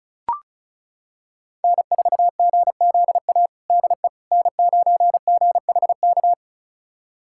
Morse code, heard as a keyed tone, about 36 wpm, 700 Hz. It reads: N4GZA DE N9GHK
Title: USA Callsign DE Callsign - 5 Characters - 35wpm